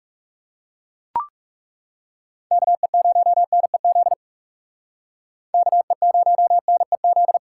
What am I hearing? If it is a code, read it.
Morse code, heard as a keyed tone, about 37 wpm, 700 Hz. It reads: KE0NEZ KE0NEZ
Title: Speed Racing - USA 2x3 Call Signs 30wpm